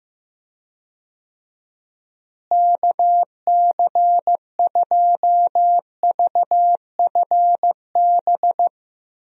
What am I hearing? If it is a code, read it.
Morse code, heard as a keyed tone, about 15 wpm, 700 Hz. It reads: KC2VFB
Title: USA Call Signs - 6 Characters - Repeated 2x - 15wpm